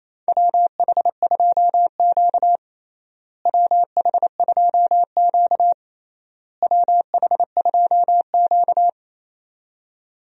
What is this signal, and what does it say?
Morse code, heard as a keyed tone, about 28 wpm, 700 Hz. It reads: WH2Q WH2Q WH2Q
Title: USA Call Signs - 4 Characters - Repeated 3x - 28wpm